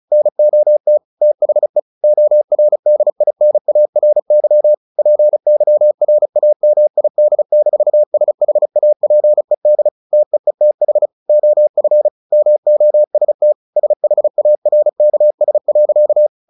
Morse code, heard as a keyed tone, about 34 wpm, 600 Hz. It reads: NOT THE ORDINARY PYRAMID-SHAPED TEETH OF MOST SHARKS.
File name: the old man and the sea 2 35wpm